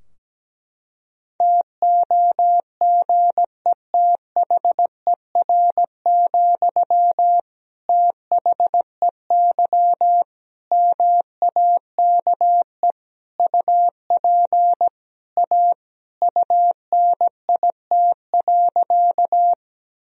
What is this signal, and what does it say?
Morse code, heard as a keyed tone, about 17 wpm, 700 Hz. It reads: TOGETHER, THEY MAKE UP A UNIT.
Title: Sentences from Top 1000 Words - Review Part 2 - 17wpm